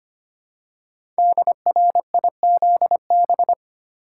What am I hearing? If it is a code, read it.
Morse code, heard as a keyed tone, about 25 wpm, 700 Hz. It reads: DRIZB
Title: Code Groups - 5 Letters - 25wpm